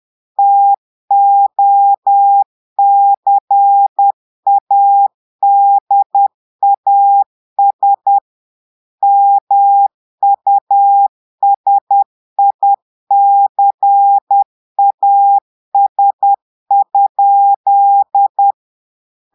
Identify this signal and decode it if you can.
Morse code, heard as a keyed tone, about 10 wpm, 800 Hz. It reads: TOCADAS MUSICAS?
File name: portugais texte en clair 2 vitesse 10 mots minutes